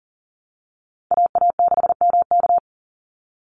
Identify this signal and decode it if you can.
Morse code, heard as a keyed tone, about 40 wpm, 700 Hz. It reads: AA6MK
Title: Call Signs Repeated 2x - 40wpm